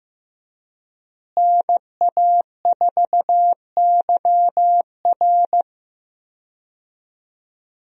Morse code, heard as a keyed tone, about 14 wpm, 700 Hz. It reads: NA4YR
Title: USA Call Signs - 5 Characters - Repeated 3x - 15wpm